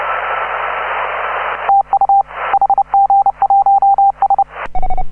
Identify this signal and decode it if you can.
Morse code, heard as a keyed tone, about 30 wpm, 780 Hz. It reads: TU HG1S